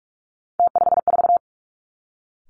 Morse code, heard as a keyed tone, about 46 wpm, 700 Hz. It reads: T54
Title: Three Challenging Letter-Number Combinations 45wpm